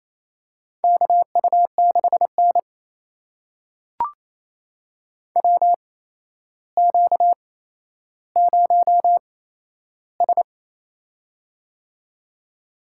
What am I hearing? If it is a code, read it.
Morse code, heard as a keyed tone, about 28 wpm, spaced as for 12 wpm, 700 Hz. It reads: KU6N WQ0S
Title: USA Call Signs - 4 Characters - 8x Character Spacing - 28wpm